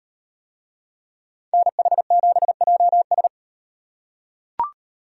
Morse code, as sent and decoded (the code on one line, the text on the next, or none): -. .... --... .--- ...
NH7JS